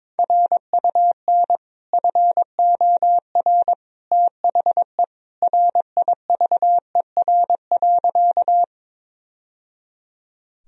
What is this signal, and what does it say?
Morse code, heard as a keyed tone, about 22 wpm, 700 Hz. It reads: RUN FOR THE RIVER.
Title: Sentences from Top 1000 Words - Review Part 1 - 22wpm